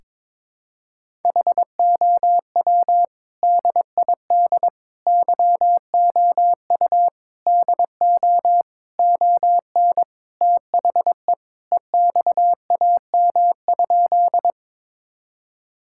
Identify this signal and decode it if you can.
Morse code, heard as a keyed tone, about 22 wpm, 700 Hz. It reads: HOW DID YOU DO ON THE EXAM?
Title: Sentences from Top 1000 Words - Review Part 5 - 22wpm